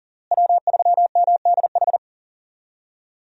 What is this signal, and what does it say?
Morse code, heard as a keyed tone, about 40 wpm, 700 Hz. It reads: W3MDH
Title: USA Call Signs - 5 Characters - Repeated 3x - 40wpm